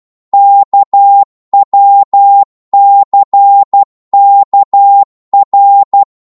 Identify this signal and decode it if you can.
Morse code, heard as a keyed tone, about 12 wpm, 800 Hz. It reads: KWCKR
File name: Morse / ltr-12wpm